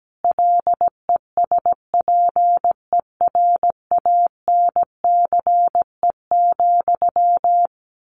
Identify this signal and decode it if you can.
Morse code, heard as a keyed tone, about 17 wpm, 700 Hz. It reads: LESPERANCE,
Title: A Sound Of Thunder 17wpm